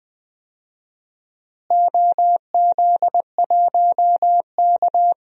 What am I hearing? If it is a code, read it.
Morse code, heard as a keyed tone, about 20 wpm, 700 Hz. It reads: OZ1K
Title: DXCC-100 Call Signs - 4 Characters - Repeated 3x - 20wpm